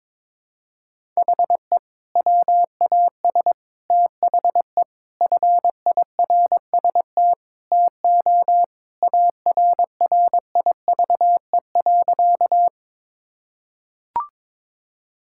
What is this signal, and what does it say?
Morse code, heard as a keyed tone, about 22 wpm, 700 Hz. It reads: HE WAS THE FIRST TO ARRIVE.
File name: practice / 22wpm / Sentences with 2-Word Phrases - 22wpm